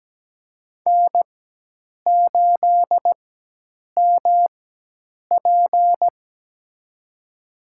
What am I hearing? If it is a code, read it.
Morse code, heard as a keyed tone, about 17 wpm, spaced as for 8 wpm, 700 Hz. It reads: N8MP
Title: USA Call Signs - 4 Characters - 4x Character Spacing - 17wpm